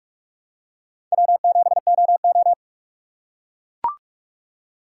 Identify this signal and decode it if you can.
Morse code, heard as a keyed tone, about 41 wpm, 700 Hz. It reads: W7OO